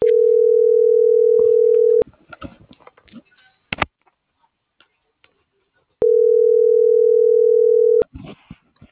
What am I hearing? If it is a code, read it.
Ringback tone (North America), 440 + 480 Hz.